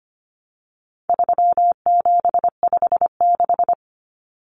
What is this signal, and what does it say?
Morse code, heard as a keyed tone, about 25 wpm, 700 Hz. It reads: 3756
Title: Common Cut Numbers - 4-Digit 25wpm